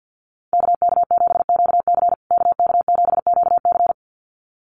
Morse code, heard as a keyed tone, about 50 wpm, 700 Hz. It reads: KK7QC KK7QC
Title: USA Call Signs Repeated 2x 50wpm